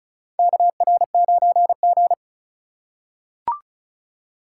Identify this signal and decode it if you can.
Morse code, heard as a keyed tone, about 34 wpm, 700 Hz. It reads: KR9G
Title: USA Call Signs - 4 Characters - 2x Character Spacing - 35wpm